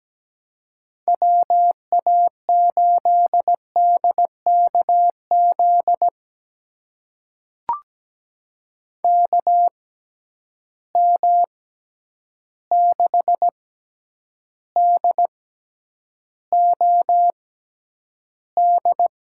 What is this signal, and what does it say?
Morse code, heard as a keyed tone, about 17 wpm, spaced as for 9 wpm, 700 Hz. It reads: WA8DKZ KM6DOD
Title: USA Call Signs - 6 Characters - 6x Character Spacing - 17wpm